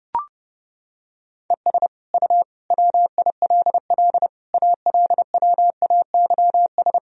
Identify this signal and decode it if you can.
Morse code, heard as a keyed tone, about 30 wpm, 700 Hz. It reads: ES U WILL ALWAYS
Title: Sentences from Top 400 Words and Abbreviations 30wpm